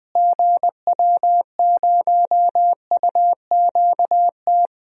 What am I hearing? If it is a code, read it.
Morse code, heard as a keyed tone, about 21 wpm, 700 Hz. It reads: GW0UQT
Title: DXCC-100 Call Signs - 6 Characters - Repeated 2x - 20wpm